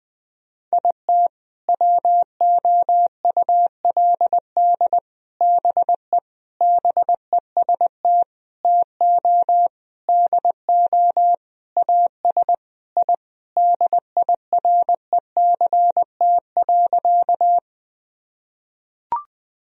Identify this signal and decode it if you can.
Morse code, heard as a keyed tone, about 20 wpm, 700 Hz. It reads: IT WOULD BE BEST TO DO AS I DIRECT.
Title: Sentences from Top 1000 Words - Review Part 2 - 20wpm